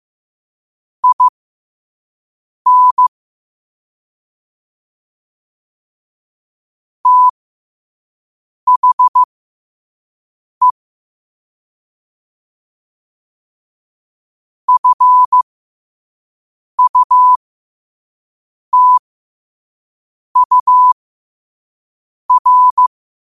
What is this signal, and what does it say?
Morse code, heard as a keyed tone, about 15 wpm, spaced as for 5 wpm, 1000 Hz. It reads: IN THE FUTUR
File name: qotdfiles / 5wpm